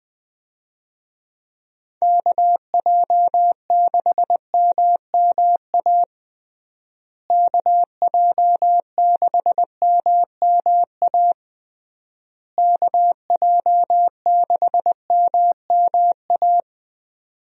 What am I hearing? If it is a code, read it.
Morse code, heard as a keyed tone, about 21 wpm, 700 Hz. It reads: KJ6MMA KJ6MMA KJ6MMA
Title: USA Call Signs - 6 Characters - Repeated 3x - 20wpm